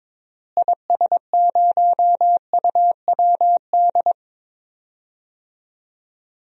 Morse code, heard as a keyed tone, about 22 wpm, 700 Hz. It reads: IS0UWD
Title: DXCC-100 Call Signs - 6 Characters - Repeated 2x - 22wpm